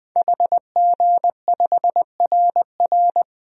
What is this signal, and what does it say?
Morse code, heard as a keyed tone, about 21 wpm, 700 Hz. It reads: HG5RR
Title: DXCC-100 Call Signs - 5 Characters - 20wpm